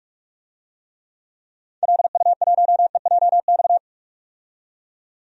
Morse code, heard as a keyed tone, about 43 wpm, 690 Hz. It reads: LU1EJX
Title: DXCC-100 Call Signs - 6 Characters - 45wpm